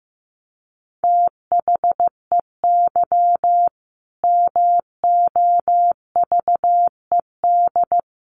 Morse code, heard as a keyed tone, about 15 wpm, 700 Hz. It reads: THEY MOVED
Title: A Sound Of Thunder 15wpm